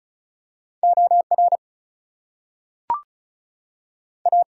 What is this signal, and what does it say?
Morse code, heard as a keyed tone, about 36 wpm, 700 Hz. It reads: OR A
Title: Two Letter Words - 35wpm